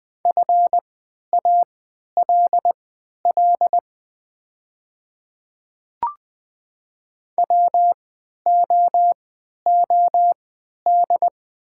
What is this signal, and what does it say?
Morse code, heard as a keyed tone, about 20 wpm, spaced as for 11 wpm, 700 Hz. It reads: FALL WOOD
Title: 3x Character Spacing - Top 500 Words - 20wpm